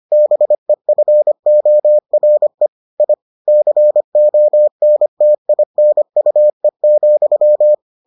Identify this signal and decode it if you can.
Morse code, heard as a keyed tone, about 25 wpm, 600 Hz. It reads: BEFORE I CONTINUE,